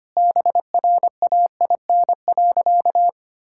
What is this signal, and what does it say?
Morse code, heard as a keyed tone, about 25 wpm, 700 Hz. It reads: BRAIN.
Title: Sentences from Top 1600 Words 25wpm